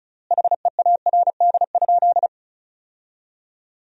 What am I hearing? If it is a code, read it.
Morse code, heard as a keyed tone, about 36 wpm, 700 Hz. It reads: HEARD?
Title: 4x Word-Spacing - Sentences from Top 400 Words - 35wpm